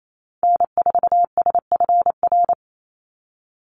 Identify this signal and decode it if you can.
Morse code, heard as a keyed tone, about 28 wpm, 700 Hz. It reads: N4SFR
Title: NA SOTA Activator Callsigns 28wpm